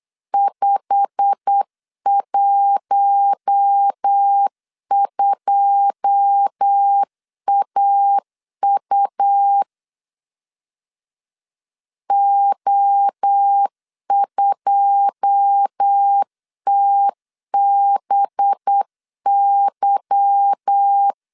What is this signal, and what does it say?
Morse code, heard as a keyed tone, about 8 wpm, 800 Hz. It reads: 512AU O2TBY
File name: LEZIONI / 9 2 8wpm 3437